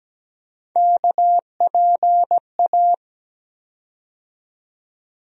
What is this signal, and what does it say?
Morse code, heard as a keyed tone, about 17 wpm, 700 Hz. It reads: KPA
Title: USA Call Sign Postfixes - 3 Letters 17wpm